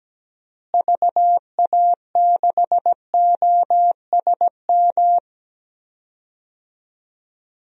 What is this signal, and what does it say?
Morse code, heard as a keyed tone, about 17 wpm, 700 Hz. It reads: VA6OSM
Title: DXCC-100 Call Signs - 6 Characters - Repeated 2x - 17wpm